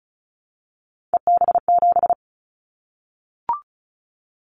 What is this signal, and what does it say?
Morse code, heard as a keyed tone, about 34 wpm, 700 Hz. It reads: EB7
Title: Callsign Prefixes for Top 25 DXCC - Repeated 3x - 35wpm